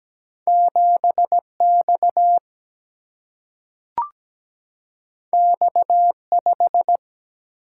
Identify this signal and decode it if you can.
Morse code, heard as a keyed tone, about 17 wpm, 700 Hz. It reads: 7X X5
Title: Single-Letter Single-Number 17wpm